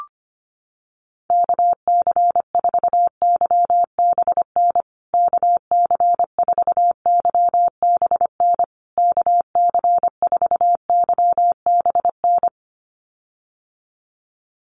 Morse code, heard as a keyed tone, about 25 wpm, 700 Hz. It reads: KC4YBN KC4YBN KC4YBN
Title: USA Call Signs Repeated 3x 25wpm